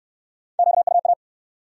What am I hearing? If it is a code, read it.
Morse code, heard as a keyed tone, about 67 wpm, 700 Hz. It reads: QUA